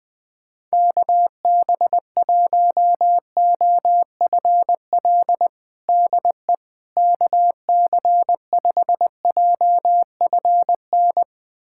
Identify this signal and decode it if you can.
Morse code, heard as a keyed tone, about 20 wpm, 700 Hz. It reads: KB1OFL DE KC5JFN
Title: USA Callsign DE Callsign - 6 Characters - 20wpm